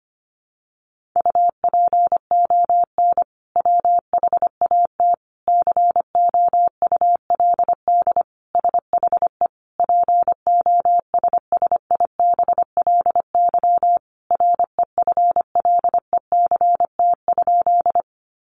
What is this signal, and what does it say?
Morse code, heard as a keyed tone, about 25 wpm, 700 Hz. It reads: UPON WHAT COULD SHE POSSIBLY REFLECT?
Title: Sentences from Top 1500 Words 25wpm